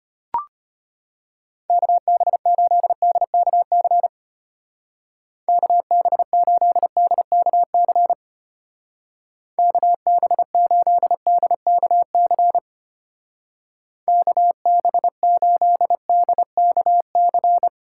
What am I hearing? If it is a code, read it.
Morse code, heard as a keyed tone, about 32 wpm, 700 Hz. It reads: KB8DKC KB8DKC KB8DKC KB8DKC